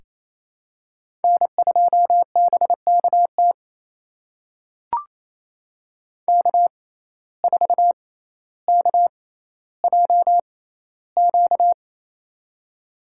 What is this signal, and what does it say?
Morse code, heard as a keyed tone, about 27 wpm, spaced as for 14 wpm, 700 Hz. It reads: N2BKT K4KJQ
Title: USA Call Signs - 5 Characters - 6x Character Spacing - 28wpm